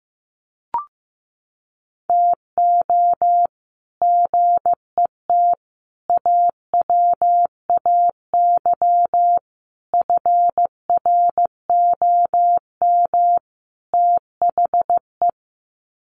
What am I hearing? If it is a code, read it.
Morse code, heard as a keyed tone, about 15 wpm, 700 Hz. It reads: TO GET AWAY FROM THE
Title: Sets of 5 Words Encore2 15wpm